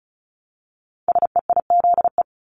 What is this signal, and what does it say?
Morse code, heard as a keyed tone, about 36 wpm, 700 Hz. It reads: SEIZE
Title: Multi-letter Phonemes Repeated 3x 35wpm